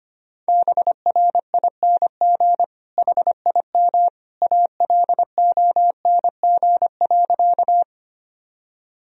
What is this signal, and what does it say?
Morse code, heard as a keyed tone, about 25 wpm, 700 Hz. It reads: BRING HIM ALONG.
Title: Sentences from Top 1000 Words - Review Part 3 - 25wpm